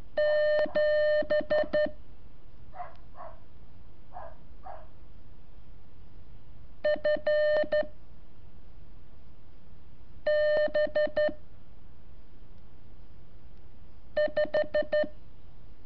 Morse code, heard as a keyed tone, about 12 wpm, 600 Hz. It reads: 7 F B 5
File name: curso / ejercicio 22